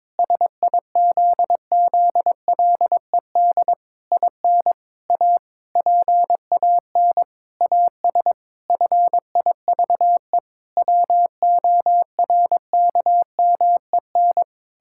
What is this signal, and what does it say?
Morse code, heard as a keyed tone, about 22 wpm, 700 Hz. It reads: SIZZLED IN A PAN AS FIVE WORKMEN
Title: Pangram Challenge v2 - 22wpm